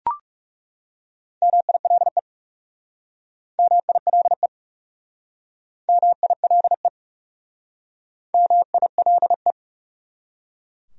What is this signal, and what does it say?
Morse code, heard as a keyed tone, about 37 wpm, 700 Hz. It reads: MILE MILE MILE MILE
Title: Speed-Racing Top 500 Words 30wpm